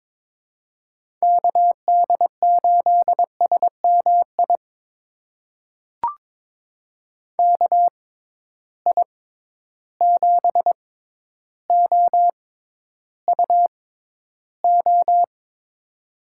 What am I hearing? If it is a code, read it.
Morse code, heard as a keyed tone, about 22 wpm, spaced as for 11 wpm, 700 Hz. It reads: KD8SMI KI7OUO
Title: USA Call Signs - 6 Characters - 6x Character Spacing - 22wpm